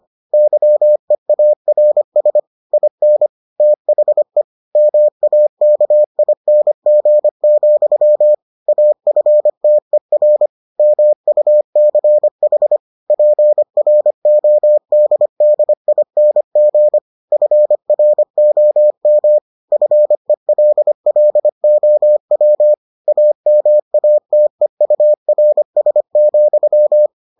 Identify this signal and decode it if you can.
Morse code, heard as a keyed tone, about 25 wpm, 600 Hz. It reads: YEARS IN THE MAKING, AFTER MUCH PRODDING FROM FELLOW AMATEURS,